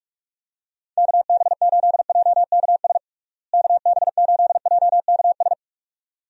Morse code, heard as a keyed tone, about 46 wpm, 700 Hz. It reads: KB8JKS KB8JKS
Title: USA Call Signs - 6 Characters - Repeated 2x - 45wpm